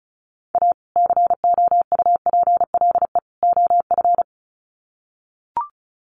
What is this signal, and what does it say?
Morse code, heard as a keyed tone, about 34 wpm, 700 Hz. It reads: A COUPLE OF
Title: Sets of 3 Words 35wpm